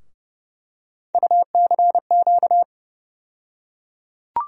Morse code, heard as a keyed tone, about 30 wpm, 700 Hz. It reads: UCQ
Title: USA Call Sign Postfixes - 3 Letters 30wpm